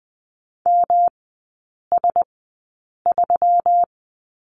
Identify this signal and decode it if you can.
Morse code, heard as a keyed tone, about 20 wpm, spaced as for 8 wpm, 700 Hz. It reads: MS3
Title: 121 Lesson 20e - Number 3 - Three Characters 20wpm